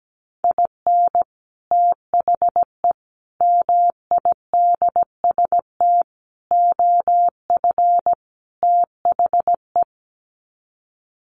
Morse code, heard as a keyed tone, about 17 wpm, 700 Hz. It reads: IN THE MIDST OF THE
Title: Sets of 5 Words 17wpm